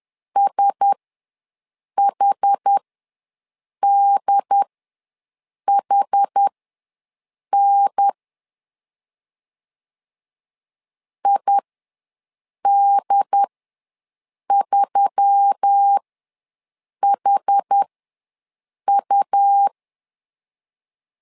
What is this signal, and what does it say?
Morse code, heard as a keyed tone, about 10 wpm, spaced as for 6 wpm, 800 Hz. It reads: SHDHN ID3HU
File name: LEZIONI / 4 Q1S3 ws 10wpm 2437